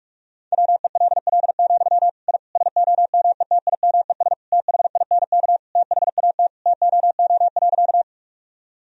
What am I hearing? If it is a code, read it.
Morse code, heard as a keyed tone, about 41 wpm, 700 Hz. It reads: WELL, I SOMETIMES THINK THAT TOO.
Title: Sentences from Top 1000 Words - Review Part 3 - 45wpm